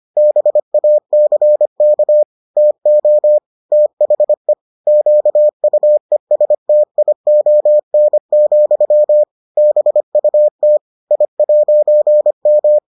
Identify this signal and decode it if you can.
Morse code, heard as a keyed tone, about 25 wpm, 600 Hz. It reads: BACK TO THE QUESTION, BUT I'M